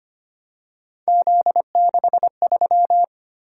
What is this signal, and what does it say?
Morse code, heard as a keyed tone, about 25 wpm, 700 Hz. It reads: Z63
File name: Three Letter-Number Combinations - 25wpm